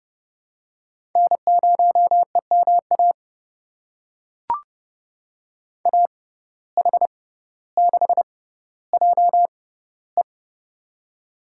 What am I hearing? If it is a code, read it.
Morse code, heard as a keyed tone, about 32 wpm, spaced as for 16 wpm, 700 Hz. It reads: N0EMA AH6JE
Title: USA Call Signs - 5 Characters - 6x Character Spacing - 30wpm